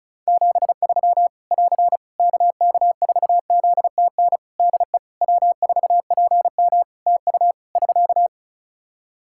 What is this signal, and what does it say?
Morse code, heard as a keyed tone, about 36 wpm, 700 Hz. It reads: 73 <AR> KK4ZTN DE W4PM TU <SK>
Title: Practice QSO - Exchange 4 - Ending - 35wpm